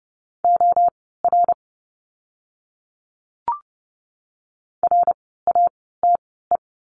Morse code, heard as a keyed tone, about 30 wpm, spaced as for 17 wpm, 700 Hz. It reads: OR RATE
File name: Lesson 17f - Letter u - Words 30wpm